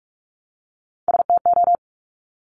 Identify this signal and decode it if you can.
Morse code, heard as a keyed tone, about 46 wpm, 700 Hz. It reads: STO
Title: Top Three Letter Combinations in Words 45wpm